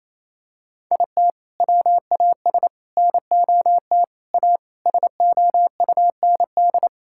Morse code, heard as a keyed tone, about 27 wpm, 700 Hz. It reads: IT WAS NOT A SOUND
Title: Sentences from Top 1000 Words - Review Part 6 - 28wpm